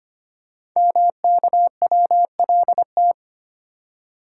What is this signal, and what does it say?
Morse code, heard as a keyed tone, about 25 wpm, 700 Hz. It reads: MKWLT
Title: Code Groups - 5 Letter:Numbers - 25wpm